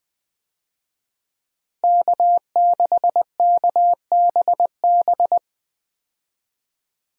Morse code, heard as a keyed tone, about 21 wpm, 700 Hz. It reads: K6KBB
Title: USA Call Signs - 5 Characters - Repeated 3x - 20wpm